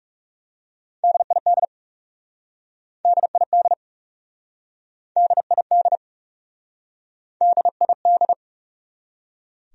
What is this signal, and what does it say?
Morse code, heard as a keyed tone, about 40 wpm, 700 Hz. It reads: DID DID DID DID